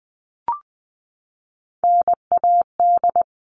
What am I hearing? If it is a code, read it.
Morse code, heard as a keyed tone, about 20 wpm, 700 Hz. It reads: NAD
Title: All Cut Numbers - 3-Digit 20wpm